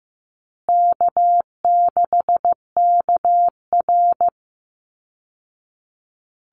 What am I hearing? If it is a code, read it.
Morse code, heard as a keyed tone, about 15 wpm, 700 Hz. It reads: K6KR
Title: Easy Call Signs Repeated 3x - 15wpm